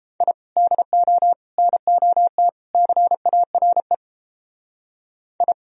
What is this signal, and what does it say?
Morse code, heard as a keyed tone, about 31 wpm, 700 Hz. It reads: I DO NOT CARE I